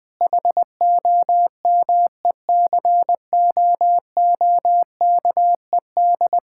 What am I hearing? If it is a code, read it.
Morse code, heard as a keyed tone, about 19 wpm, 700 Hz. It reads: HOMECOOKED
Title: Difficult Sentences with Hints - 20wpm